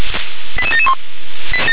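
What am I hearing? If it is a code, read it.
Five-tone selective call (ZVEI), heard as quick runs of five tones, tones of about 70 ms. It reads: 88771
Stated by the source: ZVEI five-tone selective call received by SDR.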